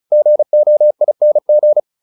Morse code, heard as a keyed tone, about 34 wpm, 600 Hz. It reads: GOING